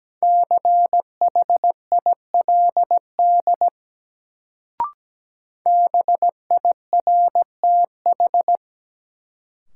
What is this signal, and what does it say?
Morse code, heard as a keyed tone, about 17 wpm, 700 Hz. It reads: CHILD BIRTH
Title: Sentences from Top 800 Words - New Word Drill - 17wpm